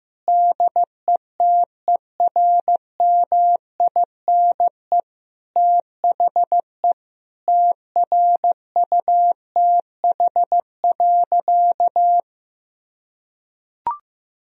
Morse code, heard as a keyed tone, about 15 wpm, 700 Hz. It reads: DETERMINE THE TRUTH.
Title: Sentences from Top 1000 Words - Review Part 4 - 15wpm